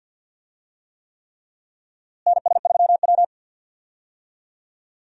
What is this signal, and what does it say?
Morse code, heard as a keyed tone, about 46 wpm, 700 Hz. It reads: NS3W
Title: USA Call Signs - 4 Characters - Repeated 3x - 50wpm